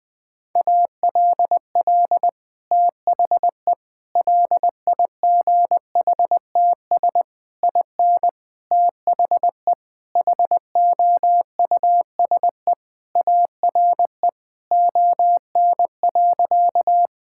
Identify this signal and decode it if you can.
Morse code, heard as a keyed tone, about 20 wpm, 700 Hz. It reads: ALL THE LIGHTS IN THE HOUSE ARE ON.
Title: Sentences with 2-Word Phrases - 20wpm